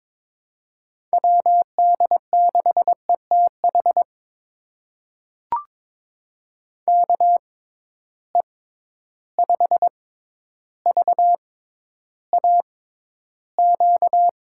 Morse code, heard as a keyed tone, about 22 wpm, spaced as for 11 wpm, 700 Hz. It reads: WD6ETH KE5VAQ
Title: USA Call Signs - 6 Characters - 6x Character Spacing - 22wpm